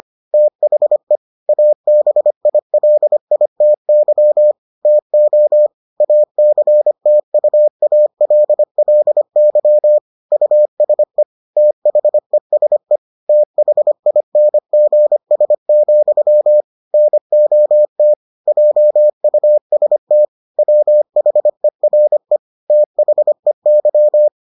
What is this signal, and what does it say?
Morse code, heard as a keyed tone, about 25 wpm, 600 Hz. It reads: THE ABILITY TO ACTUALLY USE THESE THINGS, NOT JUST WHERE THEY